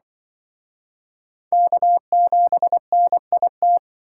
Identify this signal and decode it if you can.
Morse code, heard as a keyed tone, about 25 wpm, 700 Hz. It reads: K7NIT